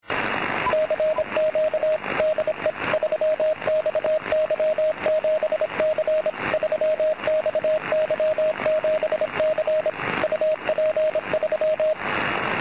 Morse code, heard as a keyed tone, about 26 wpm, 620 Hz, with radio noise. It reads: CQ DE 3XY7C 3XY7C UP3
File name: posts / 3xy7c-6m-cw